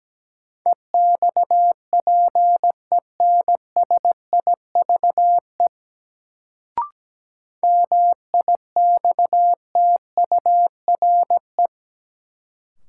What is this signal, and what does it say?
Morse code, heard as a keyed tone, about 17 wpm, 700 Hz. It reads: EXPENSIVE MIXTURE
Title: Sentences from Top 1500 Words - New Word Drill Alt - 17wpm